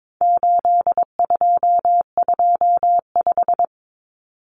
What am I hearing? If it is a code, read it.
Morse code, heard as a keyed tone, about 22 wpm, 700 Hz. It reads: 8225
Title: Numbers - 4-Digit 22wpm